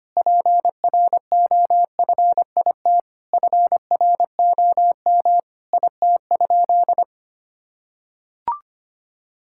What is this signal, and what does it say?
Morse code, heard as a keyed tone, about 25 wpm, 700 Hz. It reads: PROFIT FROM IT?
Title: Sentences from Top 1000 Words - Review Part 5 - 25wpm